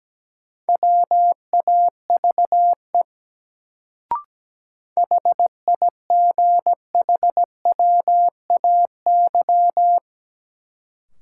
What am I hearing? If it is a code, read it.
Morse code, heard as a keyed tone, about 17 wpm, 700 Hz. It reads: WAVE HIGHWAY
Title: Sentences from Top 1400 Words - New Word Drill Alt - 17wpm